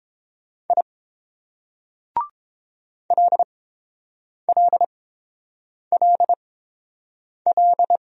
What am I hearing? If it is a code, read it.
Morse code, heard as a keyed tone, about 29 wpm, spaced as for 7 wpm, 700 Hz. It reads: I LLLL